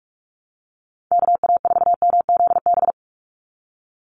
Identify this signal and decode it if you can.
Morse code, heard as a keyed tone, about 44 wpm, 700 Hz. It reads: KA4MZB
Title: USA Call Signs 45wpm